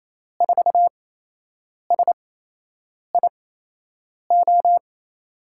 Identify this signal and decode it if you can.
Morse code, heard as a keyed tone, about 28 wpm, spaced as for 8 wpm, 700 Hz. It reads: 4SIO
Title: USA Call Signs - 6 Characters - 8x Character Spacing - 28wpm